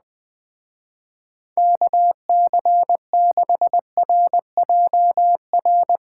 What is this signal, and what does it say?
Morse code, heard as a keyed tone, about 20 wpm, 700 Hz. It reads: KC6RJR
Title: Speed Racing - USA 2x3 Call Signs 17wpm